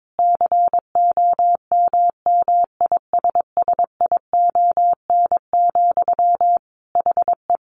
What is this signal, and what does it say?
Morse code, heard as a keyed tone, about 22 wpm, 700 Hz. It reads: COMMISSION, HE
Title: Sentences from Top 1400 Words 22wpm